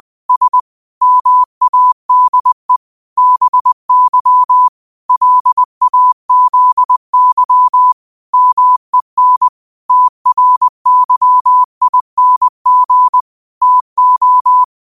Morse code, heard as a keyed tone, about 20 wpm, 1000 Hz. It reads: S MADE BY LAZY MEN TRYING TO
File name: qotdfiles / 20wpm